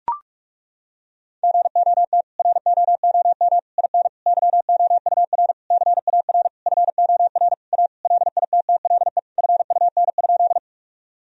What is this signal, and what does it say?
Morse code, heard as a keyed tone, about 41 wpm, 700 Hz. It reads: GOT ROOM IN YOUR CAR FOR A LITTLE FUN?
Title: Sentences from Top 1000 Words - Review Part 1 - 45wpm